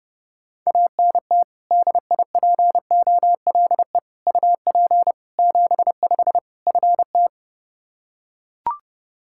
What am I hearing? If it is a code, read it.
Morse code, heard as a keyed tone, about 32 wpm, 700 Hz. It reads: ANT DIPOLE UP 75 FT
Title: Practice QSO - Exchange 2 - Info - 30wpm